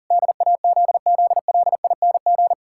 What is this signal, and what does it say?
Morse code, heard as a keyed tone, about 40 wpm, 700 Hz. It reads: DAZZLING
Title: Pangram Challenge v2 - 40wpm